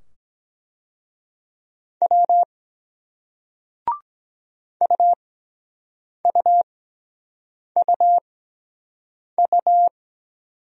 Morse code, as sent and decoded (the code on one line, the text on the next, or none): .-- / ..- ..- ..- ..-
W UUUU